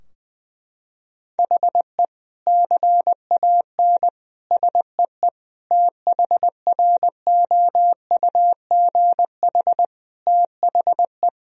Morse code, heard as a keyed tone, about 19 wpm, 700 Hz. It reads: HE CAN SEE THROUGH THE
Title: Sentences from Top 1000 Words - Review Part 2 - 20wpm